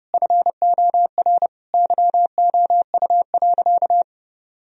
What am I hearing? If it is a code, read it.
Morse code, heard as a keyed tone, about 30 wpm, 700 Hz. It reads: FOR YOU.
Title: Sentences from Top 1000 Words - Review Part 3 - 30wpm